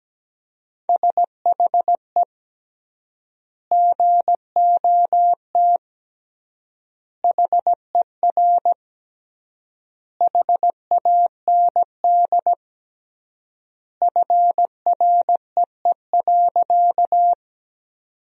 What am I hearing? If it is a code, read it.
Morse code, heard as a keyed tone, about 17 wpm, 700 Hz. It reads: SHE GOT HER HAND FREE.
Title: 3x Word-Spacing - Sentences from Top 500 Words - 17wpm